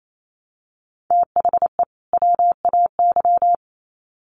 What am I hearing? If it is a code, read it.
Morse code, heard as a keyed tone, about 28 wpm, 700 Hz. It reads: THE WAY
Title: Sets of 2 Words 28wpm